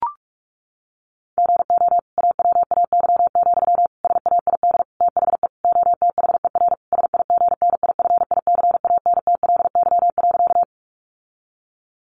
Morse code, heard as a keyed tone, about 44 wpm, 700 Hz. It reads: GO AWAY, SAID THE OTHER SIGNIFICANTLY.
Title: Sentences from Top 1500 Words 45wpm